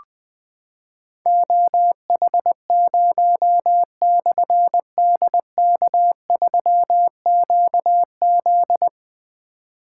Morse code, heard as a keyed tone, about 21 wpm, 700 Hz. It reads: OH0/DK3QZ
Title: Difficult Call Signs 20wpm